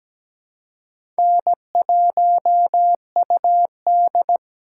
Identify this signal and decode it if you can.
Morse code, heard as a keyed tone, about 17 wpm, 700 Hz. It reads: N1UD
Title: USA Call Signs - 4 Characters - 8x Character Spacing - 17wpm